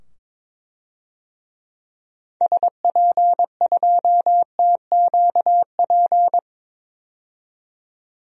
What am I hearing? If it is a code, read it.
Morse code, heard as a keyed tone, about 22 wpm, 700 Hz. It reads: SP2TQP
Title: DXCC-100 Call Signs - 6 Characters - 22wpm